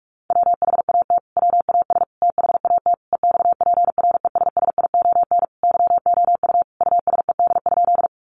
Morse code, heard as a keyed tone, about 44 wpm, 700 Hz. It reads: WHAT WAS THAT EXPRESSION YOU USED?
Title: Sentences from Top 1200 Words 45wpm